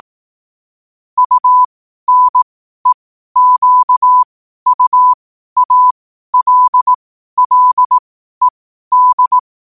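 Morse code, heard as a keyed tone, about 18 wpm, spaced as for 12 wpm, 1000 Hz. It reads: UNEQUALLED
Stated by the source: Morse at 12 wpm.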